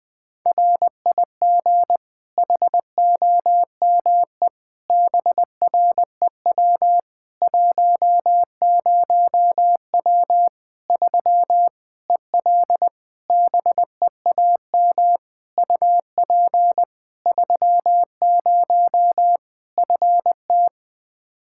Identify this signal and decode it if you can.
Morse code, heard as a keyed tone, about 20 wpm, 700 Hz. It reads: RIG HOME BREW 10W 3 EL BEAM UP 30 FT
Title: Practice QSO - Exchange 2 - Info - 20wpm